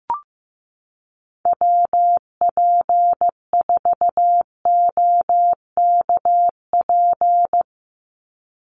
Morse code, heard as a keyed tone, about 15 wpm, 700 Hz. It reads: WP4OKP
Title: USA Call Signs 15wpm